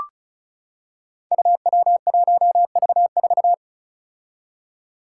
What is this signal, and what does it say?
Morse code, heard as a keyed tone, about 36 wpm, 700 Hz. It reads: UW1V4
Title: Five Challenging Letter-Number Combinations 35wpm